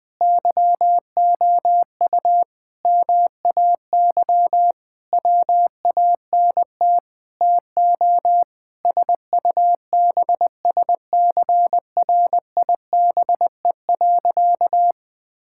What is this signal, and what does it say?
Morse code, heard as a keyed tone, about 20 wpm, 700 Hz. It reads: YOU MAY WANT TO SUBSCRIBE.
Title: Sentences with 8-Word Phrases - Spoken Beginning - 20wpm